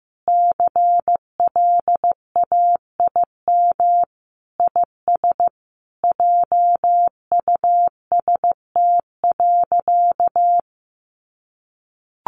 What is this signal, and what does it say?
Morse code, heard as a keyed tone, about 15 wpm, 700 Hz. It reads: CLAIM IS JUST.
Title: Sentences from Top 1500 Words 15wpm